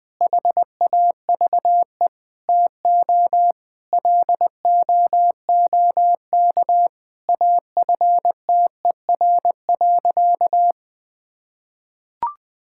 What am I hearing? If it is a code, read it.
Morse code, heard as a keyed tone, about 20 wpm, 700 Hz. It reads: HAVE TO LOOK AFTER.
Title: Sentences from Top 1600 Words 20wpm